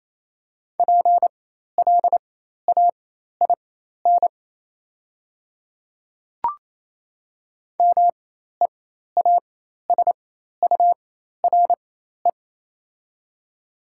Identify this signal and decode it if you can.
Morse code, heard as a keyed tone, about 28 wpm, spaced as for 13 wpm, 700 Hz. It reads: PLAIN MEASURE
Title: 4x Character Spacing - Top 500 Words - 28wpm